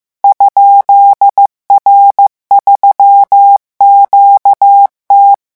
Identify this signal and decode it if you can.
Morse code, heard as a keyed tone, about 15 wpm, 800 Hz. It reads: ?R3QT
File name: morse mp3 / revision12-vitesse 14 mots